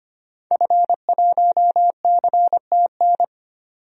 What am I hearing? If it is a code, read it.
Morse code, heard as a keyed tone, about 25 wpm, 700 Hz. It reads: F1CTN
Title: DXCC-100 Call Signs - 5 Characters - 25wpm